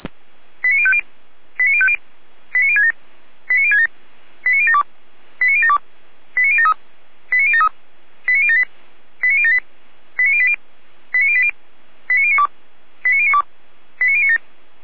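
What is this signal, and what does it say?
Five-tone selective call (ZVEI), heard as quick runs of five tones, tones of about 70 ms. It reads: 89050 89050 89067 89067 89072 89072 89073 89073 89078 89078 89080 89080 89092 89092 89097
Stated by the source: ZVEI five-tone selective call 89012, received on SDR.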